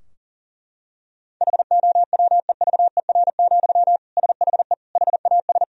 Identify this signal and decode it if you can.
Morse code, heard as a keyed tone, about 40 wpm, 700 Hz. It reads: HOWEVER, SHE HAS
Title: Sentences from Top 1000 Words - Review Part 2 - 40wpm